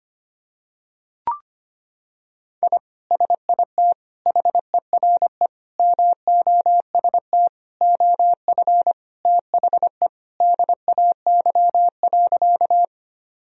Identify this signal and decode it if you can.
Morse code, heard as a keyed tone, about 25 wpm, 700 Hz. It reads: I SIT HERE MOST OF THE DAY.
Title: Sentences from Top 1000 Words - Review Part 1 - 25wpm